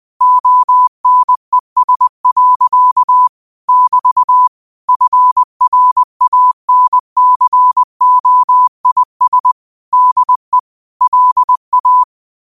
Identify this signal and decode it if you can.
Morse code, heard as a keyed tone, about 20 wpm, 1000 Hz. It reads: ONES. <BT> FRANCOIS DE LA